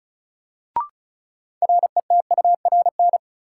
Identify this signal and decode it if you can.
Morse code, heard as a keyed tone, about 36 wpm, 700 Hz. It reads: RETURN
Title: Sentences from Top 800 Words - New Word Drill - 35wpm